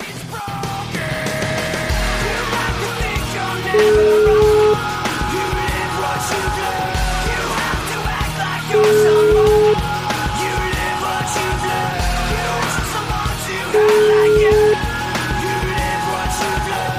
Ringback tone (Europe, 425 Hz), with noise.